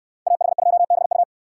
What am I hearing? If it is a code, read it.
Morse code, heard as a keyed tone, about 68 wpm, 700 Hz. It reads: AH2BU